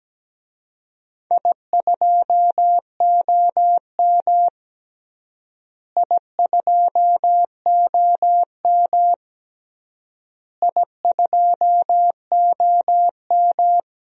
Morse code, heard as a keyed tone, about 17 wpm, 700 Hz. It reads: I2OM I2OM I2OM
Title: DXCC-100 Call Signs - 4 Characters - Repeated 3x - 17wpm